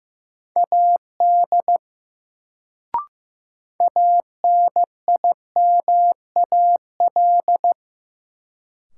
Morse code, heard as a keyed tone, about 15 wpm, 700 Hz. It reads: AD ANIMAL
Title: Sentences from Top 1300 Words - New Word Drill Alt - 15wpm